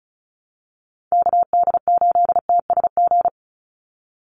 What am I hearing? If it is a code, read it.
Morse code, heard as a keyed tone, about 34 wpm, 700 Hz. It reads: KD8TSG
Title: USA Call Signs 35wpm